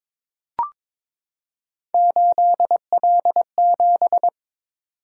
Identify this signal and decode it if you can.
Morse code, heard as a keyed tone, about 22 wpm, 700 Hz. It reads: 8L7
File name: practice / 22wpm / Three Letter-Number Combinations - 22wpm